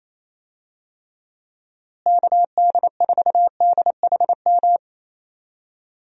Morse code, heard as a keyed tone, about 28 wpm, 700 Hz. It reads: KD4DHM
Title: USA Call Signs - 6 Characters - Repeated 2x - 28wpm